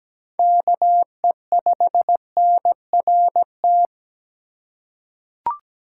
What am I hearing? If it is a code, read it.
Morse code, heard as a keyed tone, about 17 wpm, 700 Hz. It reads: KE5NRT
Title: USA Call Signs - 2x Character Spacing - 17wpm